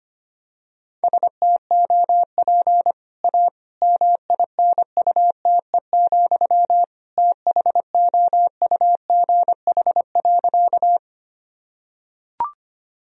Morse code, heard as a keyed tone, about 25 wpm, 700 Hz. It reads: STOP A MINUTE, THOUGH.
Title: Sentences from Top 1000 Words - Review Part 2 - 25wpm